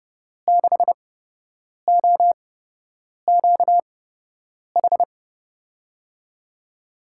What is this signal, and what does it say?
Morse code, heard as a keyed tone, about 30 wpm, spaced as for 8 wpm, 700 Hz. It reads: 6OQH
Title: USA Call Signs - 6 Characters - 8x Character Spacing - 30wpm